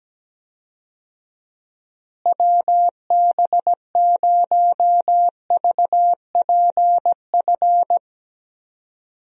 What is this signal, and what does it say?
Morse code, heard as a keyed tone, about 17 wpm, 700 Hz. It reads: WB0VPF
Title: USA Call Signs - 6 Characters - Repeated 3x - 17wpm